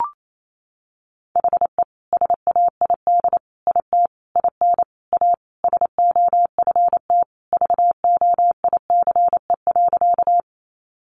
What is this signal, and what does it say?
Morse code, heard as a keyed tone, about 28 wpm, 700 Hz. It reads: HE SAID IT IN A SOFT VOICE.
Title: Sentences from Top 1500 Words 28wpm